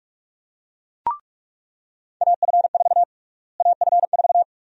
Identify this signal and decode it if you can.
Morse code, heard as a keyed tone, about 46 wpm, 700 Hz. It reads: AF4 AF4
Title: USA Call Sign Prefixes - Repeated 2x - 45wpm